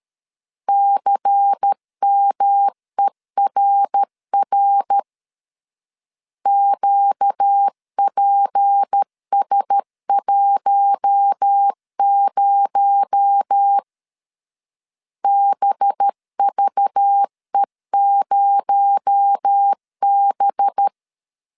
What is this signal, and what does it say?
Morse code, heard as a keyed tone, about 13 wpm, 800 Hz. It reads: CMERR QPS10 BVE0B
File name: CW / DET / 1 6 1500C S A 12wpm 3440